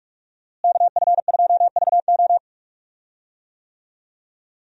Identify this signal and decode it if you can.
Morse code, heard as a keyed tone, about 46 wpm, 700 Hz. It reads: KF2VO
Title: USA Call Signs - 5 Characters - Repeated 3x - 45wpm